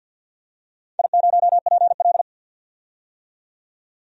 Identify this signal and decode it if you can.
Morse code, heard as a keyed tone, about 53 wpm, 700 Hz. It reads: I0PL
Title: DXCC-100 Call Signs - 4 Characters - 50wpm